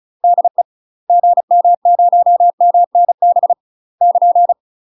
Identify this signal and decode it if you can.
Morse code, heard as a keyed tone, about 34 wpm, 700 Hz. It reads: DE GM0MNB <KN>
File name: UK MASTER QSOs TRAINING / 35WPM / UK MASTER QSOs 0025